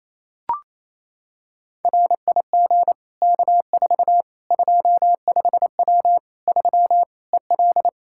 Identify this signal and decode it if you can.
Morse code, heard as a keyed tone, about 28 wpm, 700 Hz. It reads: RIG K4 25W 3 EL
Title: Practice QSO - Exchange 2 - Info - 28wpm